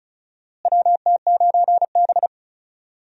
Morse code, heard as a keyed tone, about 36 wpm, 700 Hz. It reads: WT9B
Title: USA Call Signs - 4 Characters - Repeated 3x - 35wpm